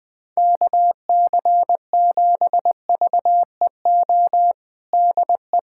Morse code, heard as a keyed tone, about 21 wpm, 700 Hz. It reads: KC7VEO DE
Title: USA Callsign DE Callsign - 6 Characters - 20wpm